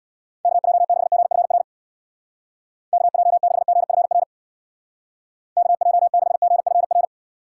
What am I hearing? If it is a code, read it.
Morse code, heard as a keyed tone, about 71 wpm, 700 Hz. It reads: KJ6GFR KJ6GFR KJ6GFR